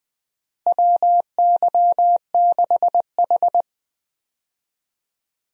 Morse code, heard as a keyed tone, about 20 wpm, 700 Hz. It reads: WY6H
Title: USA Call Signs - 4 Characters - Repeated 3x - 20wpm